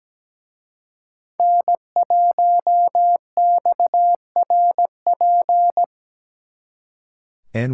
Morse code, heard as a keyed tone, about 20 wpm, 700 Hz. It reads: N1XRP
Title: Speed Racing - USA 1x3 Call Signs 17wpm